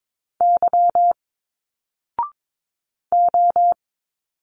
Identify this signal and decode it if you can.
Morse code, heard as a keyed tone, about 22 wpm, 700 Hz. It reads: Y O
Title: Single Letters 22wpm